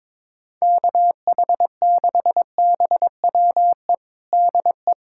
Morse code, heard as a keyed tone, about 22 wpm, 700 Hz. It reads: KH6BWE DE
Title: USA Callsign DE Callsign 22wpm